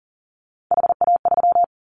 Morse code, heard as a keyed tone, about 40 wpm, 700 Hz. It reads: HA3
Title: Callsign Prefixes for Top 25 DXCC - 40wpm